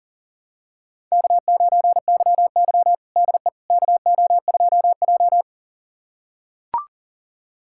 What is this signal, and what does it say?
Morse code, heard as a keyed tone, about 40 wpm, 700 Hz. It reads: K9YY DE KO2J
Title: USA Callsign DE Callsign - 4 Characters - 40wpm